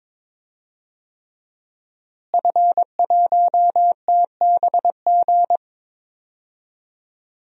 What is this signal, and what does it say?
Morse code, heard as a keyed tone, about 22 wpm, 700 Hz. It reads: F1TBG
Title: DXCC-100 Call Signs - 5 Characters - Repeated 3x - 22wpm